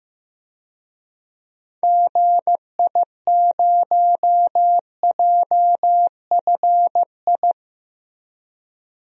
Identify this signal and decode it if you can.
Morse code, heard as a keyed tone, about 15 wpm, 700 Hz. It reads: GI0JFI
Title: DXCC-100 Call Signs - 6 Characters - Repeated 3x - 15wpm